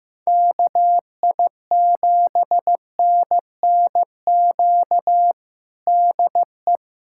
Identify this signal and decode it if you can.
Morse code, heard as a keyed tone, about 16 wpm, 700 Hz. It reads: KI7NNQ DE
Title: USA Callsign DE Callsign - 6 Characters - 15wpm